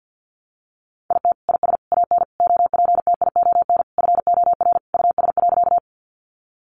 Morse code, heard as a keyed tone, about 50 wpm, 700 Hz. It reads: IT IS AN OPTION FOR US.
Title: Sentences from Top 1300 Words 50wpm